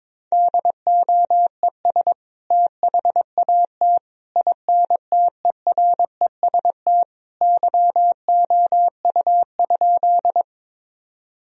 Morse code, heard as a keyed tone, about 22 wpm, 700 Hz. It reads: DOES THAT INTEREST YOU?
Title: Sentences from Top 1000 Words - Review Part 2 - 22wpm